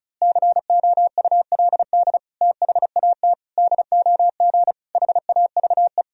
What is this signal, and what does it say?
Morse code, heard as a keyed tone, about 36 wpm, 700 Hz. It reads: COULD THAT DOG HAVE
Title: Sentences from Top 1000 Words - Review Part 2 - 35wpm